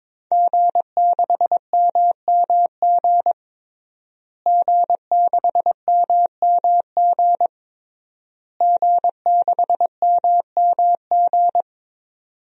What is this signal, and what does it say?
Morse code, heard as a keyed tone, about 22 wpm, 700 Hz. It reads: G6MMG G6MMG G6MMG
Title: DXCC-100 Call Signs - 5 Characters - Repeated 3x - 22wpm